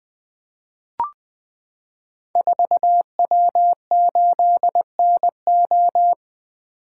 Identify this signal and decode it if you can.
Morse code, heard as a keyed tone, about 20 wpm, 700 Hz. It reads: 4W8NO
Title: Code Groups - 5 Letter:Numbers - 20wpm